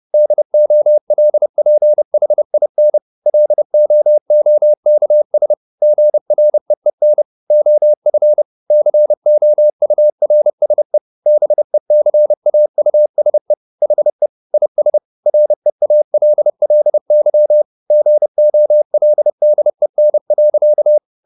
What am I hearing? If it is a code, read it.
Morse code, heard as a keyed tone, about 30 wpm, 600 Hz. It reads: DOLPHIN LOOKS GREEN OF COURSE BECAUSE HE IS REALLY GOLDEN.